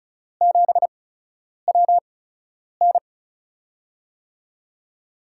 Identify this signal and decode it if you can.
Morse code, heard as a keyed tone, about 36 wpm, spaced as for 10 wpm, 700 Hz. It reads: 7WN
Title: USA Call Signs - 5 Characters - 8x Character Spacing - 35wpm